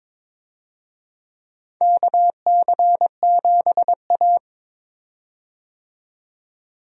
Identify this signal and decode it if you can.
Morse code, heard as a keyed tone, about 22 wpm, 700 Hz. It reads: KC7A
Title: USA Call Signs - 4 Characters - 22wpm